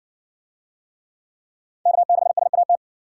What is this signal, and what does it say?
Morse code, heard as a keyed tone, about 60 wpm, 700 Hz. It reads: K6SAT